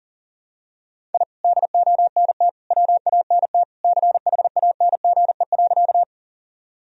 Morse code, heard as a keyed tone, about 40 wpm, 700 Hz. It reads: I DONT WANT CHANGE.
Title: Sentences from Top 1000 Words - Review Part 1 - 40wpm